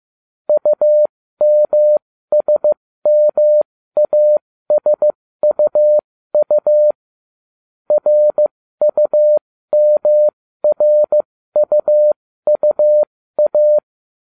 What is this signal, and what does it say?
Morse code, heard as a keyed tone, about 15 wpm, spaced as for 13 wpm, 600 Hz. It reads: UMSMASUU RUMRUUA
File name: ubacw05 2